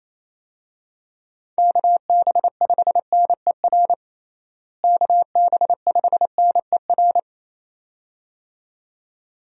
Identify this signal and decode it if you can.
Morse code, heard as a keyed tone, about 28 wpm, 700 Hz. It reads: KB5NER KB5NER
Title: USA Call Signs - 6 Characters - Repeated 2x - 28wpm